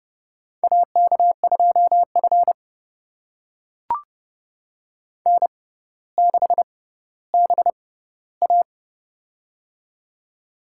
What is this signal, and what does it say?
Morse code, heard as a keyed tone, about 30 wpm, spaced as for 15 wpm, 700 Hz. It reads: AK2F N6BA
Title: USA Call Signs - 4 Characters - 6x Character Spacing - 30wpm